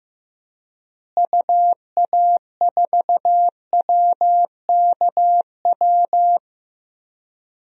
Morse code, heard as a keyed tone, about 15 wpm, 700 Hz. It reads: UA4WKW
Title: DXCC-100 Call Signs - 6 Characters - Repeated 3x - 15wpm